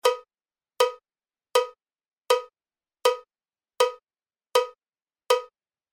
There is no code.